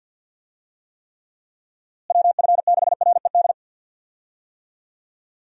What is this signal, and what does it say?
Morse code, heard as a keyed tone, about 47 wpm, 700 Hz. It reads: WF6RED
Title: USA Call Signs - 6 Characters - Repeated 3x - 50wpm